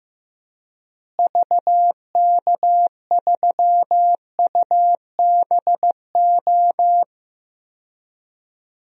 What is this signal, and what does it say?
Morse code, heard as a keyed tone, about 15 wpm, 700 Hz. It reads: VK3UBO
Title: DXCC-100 Call Signs - 6 Characters - Repeated 2x - 15wpm